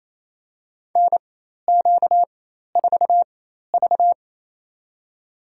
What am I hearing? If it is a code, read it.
Morse code, heard as a keyed tone, about 28 wpm, spaced as for 13 wpm, 700 Hz. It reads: NQ4V
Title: USA Call Signs - 4x Character Spacing - 28wpm